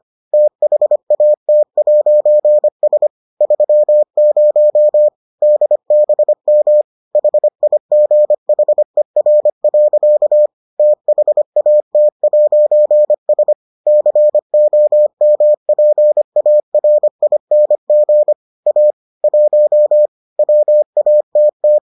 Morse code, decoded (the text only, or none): THAT'S 30 DBM HIGHER. THAT'S COMPARING A 1 WATT